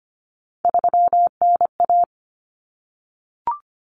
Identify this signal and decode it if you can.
Morse code, heard as a keyed tone, about 25 wpm, 700 Hz. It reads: 3NA
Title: Common Cut Numbers - 3-Digit 25wpm